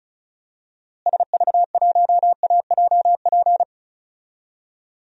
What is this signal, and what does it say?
Morse code, heard as a keyed tone, about 34 wpm, 700 Hz. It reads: SV1AJP
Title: DXCC-100 Call Signs - 6 Characters - Repeated 3x - 35wpm